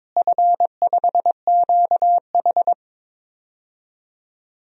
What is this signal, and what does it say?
Morse code, heard as a keyed tone, about 22 wpm, 700 Hz. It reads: F5QH
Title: DXCC-100 Call Signs - 4 Characters - Repeated 3x - 22wpm